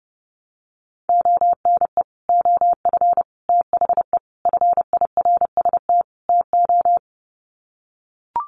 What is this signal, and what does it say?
Morse code, heard as a keyed tone, about 30 wpm, 700 Hz. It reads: ONE OF THE FIRST TO
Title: Sets of 5 Words Encore 30wpm